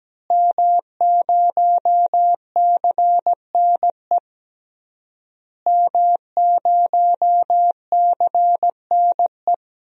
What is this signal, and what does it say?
Morse code, heard as a keyed tone, about 17 wpm, 700 Hz. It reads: M0CNE M0CNE
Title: DXCC-100 Call Signs - 5 Characters - Repeated 3x - 17wpm